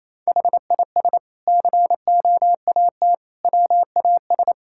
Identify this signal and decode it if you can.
Morse code, heard as a keyed tone, about 28 wpm, 700 Hz. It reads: HIS COAT WAS
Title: Sentences from Top 1600 Words 28wpm